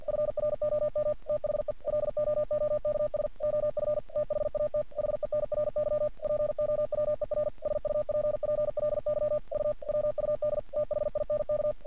Morse code, heard as a keyed tone, about 44 wpm, 600 Hz. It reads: FROM THE LOOKS OF THAT HENRY POWER SUPPLY U WUD THINK
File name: archived / CRSnet